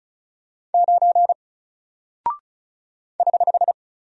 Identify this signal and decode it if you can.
Morse code, heard as a keyed tone, about 36 wpm, 700 Hz. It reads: 9 <HH>
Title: Single Character (Letter:Number:Punctuation:Prosign) Alt 35wpm